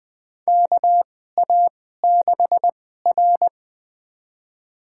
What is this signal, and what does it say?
Morse code, heard as a keyed tone, about 21 wpm, spaced as for 15 wpm, 700 Hz. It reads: KA6R
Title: USA Call Signs - 2x Character Spacing - 20wpm